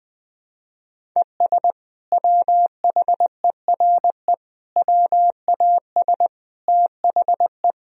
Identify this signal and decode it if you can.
Morse code, heard as a keyed tone, about 20 wpm, 700 Hz. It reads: ES WHERE WAS THE
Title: Sentences from Top 200 Words and Abbreviations 20wpm